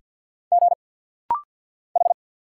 Morse code, heard as a keyed tone, about 48 wpm, 700 Hz. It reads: G H
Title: Single Letter-Number - Rapid-Fire 50wpm